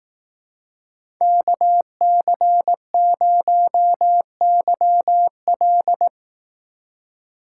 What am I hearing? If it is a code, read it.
Morse code, heard as a keyed tone, about 18 wpm, 700 Hz. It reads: KC0YL
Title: Speed Racing - USA 2x2 Call Signs 15wpm